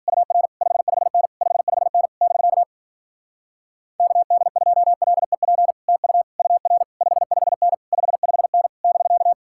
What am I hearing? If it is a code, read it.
Morse code, heard as a keyed tone, about 49 wpm, 700 Hz. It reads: UR 55N 55N <BK> KD2LEP TU UR 55N 55N <BK>
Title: Speed-Racing Common QSO Phrases - 40wpm